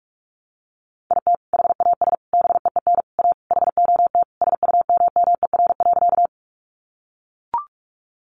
Morse code, heard as a keyed tone, about 44 wpm, 700 Hz. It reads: IT HAS BEEN A HOT SUMMER.
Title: Sentences from Top 1100 Words 45wpm